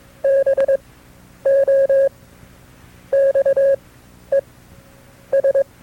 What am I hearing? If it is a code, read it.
Morse code, heard as a keyed tone, about 22 wpm, spaced as for 9 wpm, 540 Hz. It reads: BOXES
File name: FreeFiles / CW-practice / KB6MT-tape2-side2